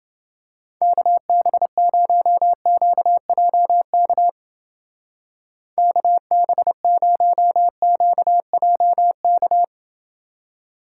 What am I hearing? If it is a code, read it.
Morse code, heard as a keyed tone, about 28 wpm, 700 Hz. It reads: KB0QJK KB0QJK